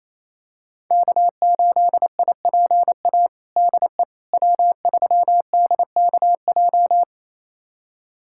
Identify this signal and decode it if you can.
Morse code, heard as a keyed tone, about 27 wpm, 700 Hz. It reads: K8IPA DE W3DKJ
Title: USA Callsign DE Callsign - 5 Characters - 28wpm